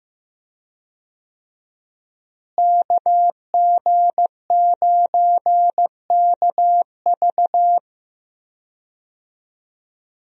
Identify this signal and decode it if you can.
Morse code, heard as a keyed tone, about 15 wpm, 700 Hz. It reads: KG9KV
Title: USA Call Signs - 5 Characters - Repeated 2x - 15wpm